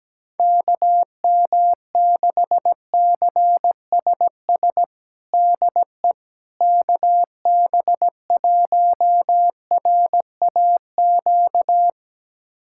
Morse code, heard as a keyed tone, about 17 wpm, 700 Hz. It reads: KM6CSS DE KB1RAQ
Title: USA Callsign DE Callsign 17wpm